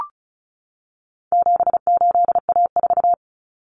Morse code, heard as a keyed tone, about 34 wpm, 700 Hz. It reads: 78A4
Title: Common Cut Numbers - 4-Digit 35wpm